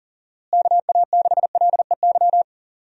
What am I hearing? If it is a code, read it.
Morse code, heard as a keyed tone, about 37 wpm, 700 Hz. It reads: KA6LEY
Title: USA Call Signs - 6 Characters - Repeated 2x - 40wpm